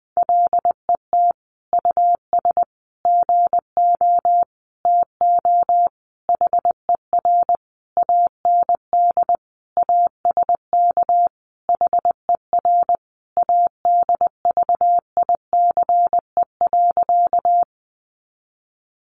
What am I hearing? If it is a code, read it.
Morse code, heard as a keyed tone, about 20 wpm, 700 Hz. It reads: LET US GO TO HER AND ASK HER ADVICE.
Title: Sentences from Top 1100 Words 20wpm